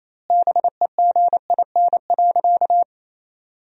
Morse code, heard as a keyed tone, about 28 wpm, 700 Hz. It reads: BEGIN.
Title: 2x Word-Spacing - Sentences from Top 300 Words - 28wpm